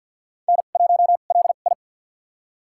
Morse code, heard as a keyed tone, about 53 wpm, spaced as for 37 wpm, 700 Hz. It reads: N1LI
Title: USA Call Signs - 4 Characters - 2x Character Spacing - 50wpm